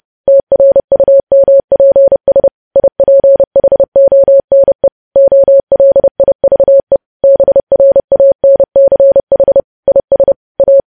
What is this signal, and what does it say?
Morse code, heard as a keyed tone, about 30 wpm, 550 Hz. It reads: TRUMPS IPHONE OLIVE BRANCH IS A
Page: news headlines in morse code at 30 wpm